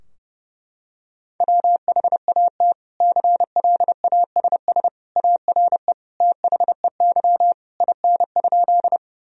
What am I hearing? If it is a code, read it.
Morse code, heard as a keyed tone, about 30 wpm, 700 Hz. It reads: WHAT CLASS ARE THEY IN?
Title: Sentences from Top 1000 Words - Review Part 2 - 30wpm